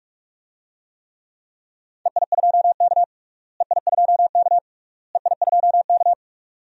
Morse code, heard as a keyed tone, about 46 wpm, 700 Hz. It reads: EI2K EI2K EI2K
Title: DXCC-100 Call Signs - 4 Characters - Repeated 3x - 45wpm